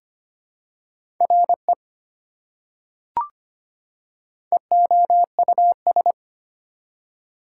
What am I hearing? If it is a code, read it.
Morse code, heard as a keyed tone, about 25 wpm, 700 Hz. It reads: RE EOUS
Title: Prefixes and Suffixes 25wpm